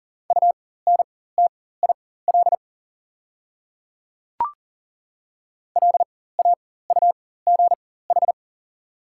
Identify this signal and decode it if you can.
Morse code, heard as a keyed tone, about 37 wpm, spaced as for 18 wpm, 700 Hz. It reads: UNTIL LAUGH
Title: 4x Character Spacing - Top 500 Words - 40wpm